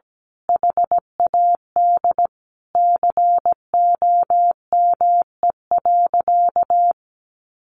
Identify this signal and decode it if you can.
Morse code, heard as a keyed tone, about 17 wpm, 700 Hz. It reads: HAD COME.
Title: Sentences from Top 1500 Words 17wpm